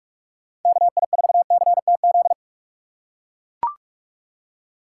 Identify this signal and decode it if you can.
Morse code, heard as a keyed tone, about 44 wpm, 700 Hz. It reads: KI4CTZ